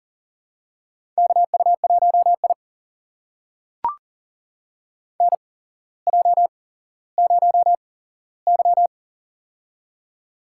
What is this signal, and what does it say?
Morse code, heard as a keyed tone, about 40 wpm, spaced as for 17 wpm, 700 Hz. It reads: KU1I NJ0Y
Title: USA Call Signs - 4 Characters - 8x Character Spacing - 40wpm